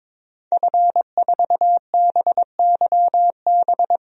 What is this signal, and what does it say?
Morse code, heard as a keyed tone, about 21 wpm, 700 Hz. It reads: F4BYB
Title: DXCC-100 Call Signs - 5 Characters - 22wpm